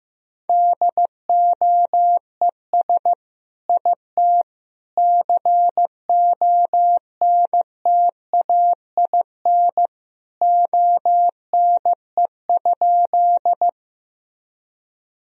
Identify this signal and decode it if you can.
Morse code, heard as a keyed tone, about 15 wpm, 700 Hz. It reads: DOES IT CONTAIN ONE?
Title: Sentences from Top 1000 Words - Review Part 2 - 15wpm